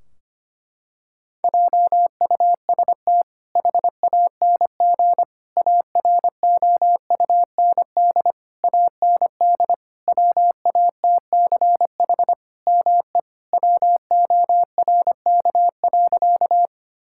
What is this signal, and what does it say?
Morse code, heard as a keyed tone, about 25 wpm, 700 Hz. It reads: JUST HANG AROUND AND WATCH ME WORK.
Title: Sentences from Top 1600 Words 25wpm